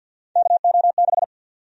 Morse code, heard as a keyed tone, about 51 wpm, 700 Hz. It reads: KO6
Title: USA Call Sign Prefixes - 50wpm